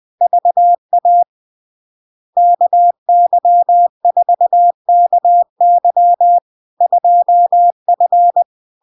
Morse code, heard as a keyed tone, about 20 wpm, 700 Hz. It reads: VA KY4KY 2F
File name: LARK-FD-2022-Log-in-CW-20wpm-700-Hz